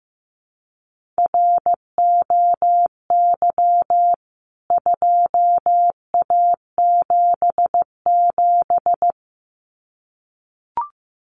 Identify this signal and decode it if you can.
Morse code, heard as a keyed tone, about 15 wpm, 700 Hz. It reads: ROY 2A77
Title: CWT Exchange 15wpm